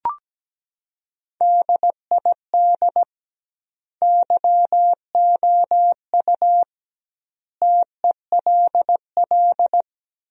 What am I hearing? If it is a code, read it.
Morse code, heard as a keyed tone, about 17 wpm, 700 Hz. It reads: DID YOU TELL
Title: 2x Word-Spacing - Sentences from Top 400 Words - 17wpm